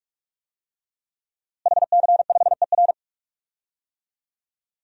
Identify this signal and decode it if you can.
Morse code, heard as a keyed tone, about 46 wpm, 700 Hz. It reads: HC5ER
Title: DXCC-100 Call Signs - 5 Characters - Repeated 2x - 45wpm